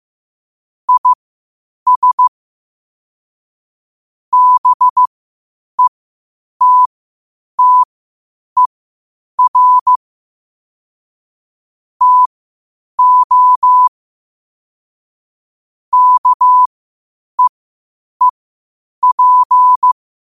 Morse code, heard as a keyed tone, about 15 wpm, spaced as for 8 wpm, 1000 Hz. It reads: IS BETTER TO KEEP
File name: qotdfiles / 8wpm